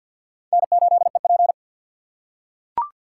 Morse code, heard as a keyed tone, about 45 wpm, 700 Hz. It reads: N8EP
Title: USA Call Signs - 2x Character Spacing - 50wpm